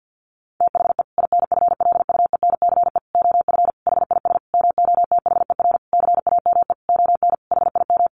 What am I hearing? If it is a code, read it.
Morse code, heard as a keyed tone, about 50 wpm, 700 Hz. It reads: THE INFLUENCE OF HIS MOTHER CAME ON HIM
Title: Sentences from Top 1100 Words 50wpm